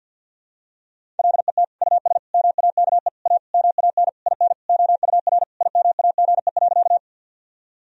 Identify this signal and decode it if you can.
Morse code, heard as a keyed tone, about 50 wpm, 700 Hz. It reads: LET US MAKE A MAN IN OUR IMAGE.
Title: Sentences from Top 1000 Words - Review Part 4 - 50wpm